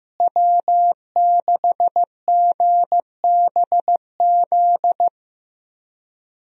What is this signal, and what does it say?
Morse code, heard as a keyed tone, about 15 wpm, 700 Hz. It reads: W6GBZ
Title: USA Call Signs - 5 Characters - Repeated 2x - 15wpm